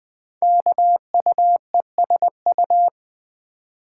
Morse code, heard as a keyed tone, about 21 wpm, 700 Hz. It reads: KUESU
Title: Code Groups - 5 Letter:Numbers - 20wpm